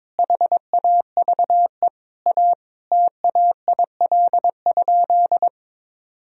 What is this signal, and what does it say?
Morse code, heard as a keyed tone, about 22 wpm, 700 Hz. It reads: HAVE A TAIL?
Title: Sentences from Top 1000 Words - Review Part 3 - 22wpm